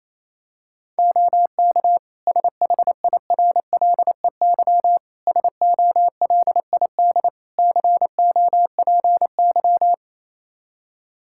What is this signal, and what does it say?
Morse code, heard as a keyed tone, about 28 wpm, 700 Hz. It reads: OK SHIRLEY SOLID COPY
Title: Practice QSO - Exchange 3 - More Info - 28wpm